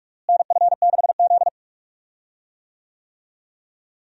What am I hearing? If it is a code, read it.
Morse code, heard as a keyed tone, about 44 wpm, 700 Hz. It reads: NF6Z
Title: USA Call Signs - 4 Characters - Repeated 3x - 45wpm